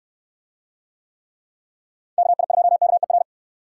Morse code, heard as a keyed tone, about 69 wpm, 700 Hz. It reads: KI2KER